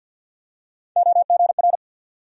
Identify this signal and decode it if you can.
Morse code, heard as a keyed tone, about 47 wpm, 700 Hz. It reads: OGR
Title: USA Call Sign Postfixes - 3 Letters 50wpm